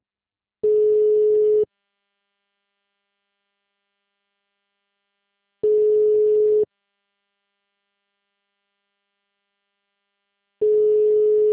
Ringback tone (Europe, 425 Hz).